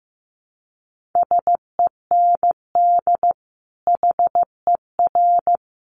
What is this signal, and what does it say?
Morse code, heard as a keyed tone, about 15 wpm, 700 Hz. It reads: SEND HER
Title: Sentences from Top 1500 Words 15wpm